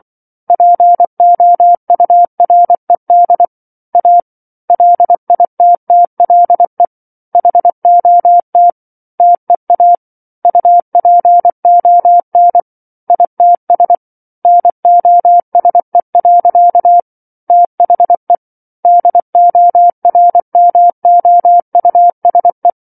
Morse code, heard as a keyed tone, about 24 wpm, 700 Hz. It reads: POURED A LITTLE HOT TEA UPON ITS NOSE. THE DORMOUSE